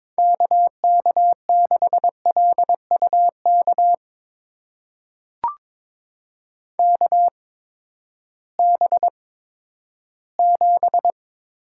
Morse code, heard as a keyed tone, about 21 wpm, spaced as for 12 wpm, 700 Hz. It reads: KK6LUK KB7
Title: USA Call Signs - 6 Characters - 8x Character Spacing - 22wpm